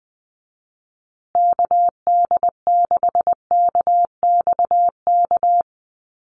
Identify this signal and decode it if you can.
Morse code, heard as a keyed tone, about 19 wpm, 700 Hz. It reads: KD6KXK
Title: USA Call Signs Repeated 2x 20wpm